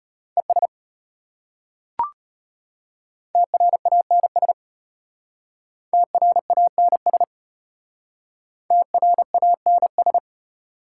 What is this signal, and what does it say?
Morse code, heard as a keyed tone, about 32 wpm, 700 Hz. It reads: ES TRANS TRANS TRANS